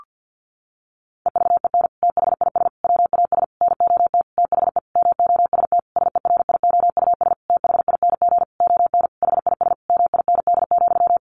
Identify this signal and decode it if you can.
Morse code, heard as a keyed tone, about 50 wpm, 700 Hz. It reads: EVEN THIS WAS NOT THE MOST SERIOUS THING ON HIS MIND,
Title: Sentences from Top 1100 Words 50wpm